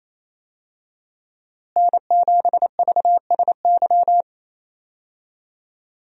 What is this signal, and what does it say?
Morse code, heard as a keyed tone, about 28 wpm, 700 Hz. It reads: N7VSY
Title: USA Call Signs - 5 Characters - Repeated 3x - 28wpm